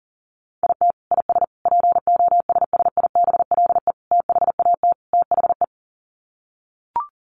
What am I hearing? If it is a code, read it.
Morse code, heard as a keyed tone, about 40 wpm, 700 Hz. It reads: IT IS POSSIBLE THAT THE
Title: Sets of 5 Words Encore2 40wpm